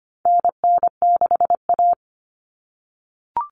Morse code, heard as a keyed tone, about 25 wpm, 700 Hz. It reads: NN6A
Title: Common Cut Numbers - 4-Digit 25wpm